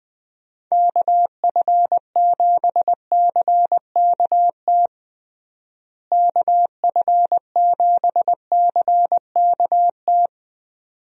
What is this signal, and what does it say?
Morse code, heard as a keyed tone, about 21 wpm, 700 Hz. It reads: KF7CKT KF7CKT
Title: USA Call Signs - 6 Characters - Repeated 2x - 20wpm